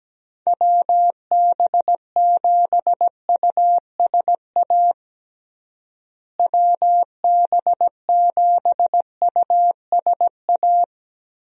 Morse code, heard as a keyed tone, about 17 wpm, 700 Hz. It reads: WB7USA WB7USA
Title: USA Call Signs - 6 Characters - Repeated 3x - 17wpm